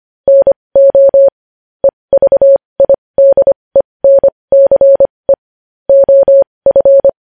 Morse code, heard as a keyed tone, about 25 wpm, spaced as for 20 wpm, 550 Hz. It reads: NO EVIDENCE OF
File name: podcast / News headlines-1x-25x20